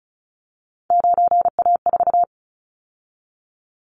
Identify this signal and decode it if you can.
Morse code, heard as a keyed tone, about 36 wpm, 690 Hz. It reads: 9A4
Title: Callsign Prefixes for Top 50 DXCC - 35wpm